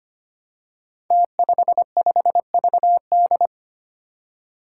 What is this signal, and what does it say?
Morse code, heard as a keyed tone, about 25 wpm, 700 Hz. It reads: T55VD
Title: Five Challenging Letter-Number Combinations 25wpm